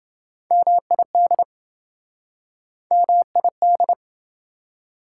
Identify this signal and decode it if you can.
Morse code, heard as a keyed tone, about 27 wpm, 700 Hz. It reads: MID MID